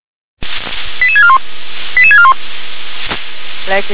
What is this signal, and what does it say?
Five-tone selective call (ZVEI), heard as quick runs of five tones, tones of about 70 ms, with speech. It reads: 88641 88641